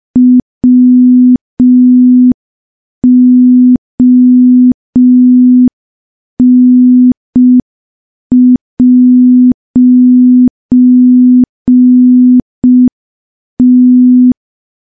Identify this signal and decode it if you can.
Morse code, heard as a keyed tone, about 5 wpm, 250 Hz. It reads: WON'T
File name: morse / morse-training-05WPM